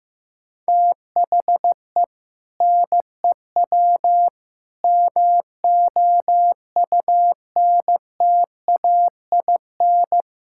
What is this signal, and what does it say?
Morse code, heard as a keyed tone, about 15 wpm, 700 Hz. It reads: THE NEW MOUNTAIN
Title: Sentences from Top 1600 Words 15wpm